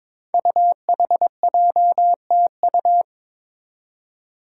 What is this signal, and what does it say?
Morse code, heard as a keyed tone, about 23 wpm, 700 Hz. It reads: UHJTU
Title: Five Challenging Letter-Number Combinations 22wpm